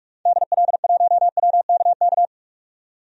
Morse code, heard as a keyed tone, about 42 wpm, 700 Hz. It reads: DL1WKK
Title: DXCC-100 Call Signs - 6 Characters - 45wpm